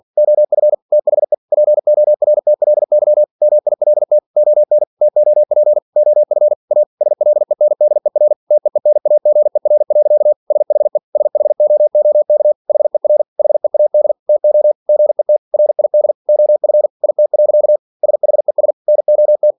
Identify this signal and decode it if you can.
Morse code, heard as a keyed tone, about 46 wpm, 600 Hz. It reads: OF THE PORTLY MILT ON TOP OF A SLENDER TEENAGER. SHE SHOOK HER HEAD TO GET RID OF IT. SHES NOT
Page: Never 47wpm